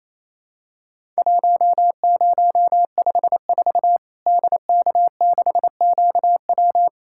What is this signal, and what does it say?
Morse code, heard as a keyed tone, about 28 wpm, 700 Hz. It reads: 1054 DK6QW
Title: Worked All Europe DX Contest - QTC Exchange - 28wpm